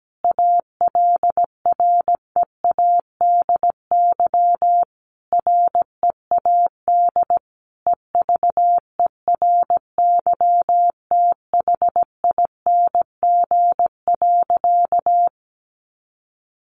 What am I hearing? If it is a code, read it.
Morse code, heard as a keyed tone, about 17 wpm, 700 Hz. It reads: ALREADY READ EVERYTHING.
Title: Sentences from Top 700 Words 17wpm